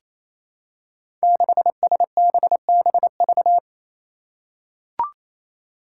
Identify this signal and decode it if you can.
Morse code, heard as a keyed tone, about 27 wpm, 700 Hz. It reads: 6SBBV
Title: Five Challenging Letter-Number Combinations 28wpm